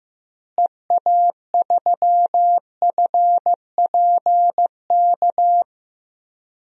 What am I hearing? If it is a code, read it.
Morse code, heard as a keyed tone, about 15 wpm, 700 Hz. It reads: EA3FPK
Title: DXCC-100 Call Signs - 6 Characters - Repeated 2x - 15wpm